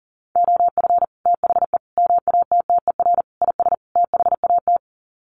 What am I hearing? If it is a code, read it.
Morse code, heard as a keyed tone, about 40 wpm, 700 Hz. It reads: OF THE MATTER IS THAT
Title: Sets of 5 Words Encore 40wpm